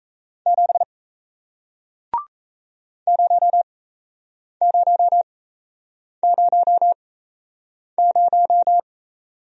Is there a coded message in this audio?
Yes: Morse code from 0.5 to 8.8 s, heard as a keyed tone, about 38 wpm, spaced as for 8 wpm, 700 Hz. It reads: 7 0000